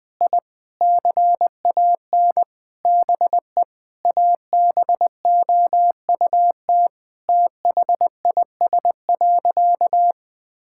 Morse code, heard as a keyed tone, about 20 wpm, 700 Hz. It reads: I CAN BE ABOUT THIS.
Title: Sentences with 9-Word Phrases - 20wpm